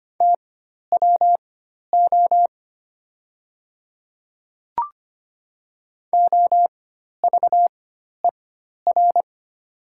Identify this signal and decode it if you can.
Morse code, heard as a keyed tone, about 25 wpm, spaced as for 12 wpm, 700 Hz. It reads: TWO OVER
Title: 4x Character Spacing - Top 200 Words - 25wpm